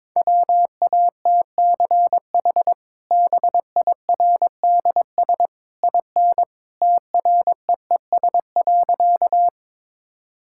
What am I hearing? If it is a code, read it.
Morse code, heard as a keyed tone, about 22 wpm, 700 Hz. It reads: WATCH BIRDS IN TREES.
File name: practice / 22wpm / Easy Sentences with Hints - 22wpm